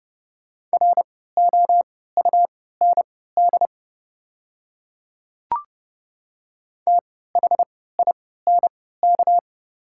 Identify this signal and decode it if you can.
Morse code, heard as a keyed tone, about 30 wpm, spaced as for 17 wpm, 700 Hz. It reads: ROUND THINK
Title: 3x Character Spacing - Top 200 Words - 30wpm